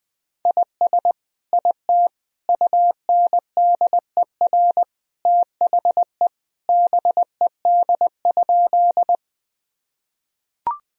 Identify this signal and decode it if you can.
Morse code, heard as a keyed tone, about 20 wpm, 700 Hz. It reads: IS IT UNDER THE BED?
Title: Sentences from Top 1000 Words - Review Part 1 - 20wpm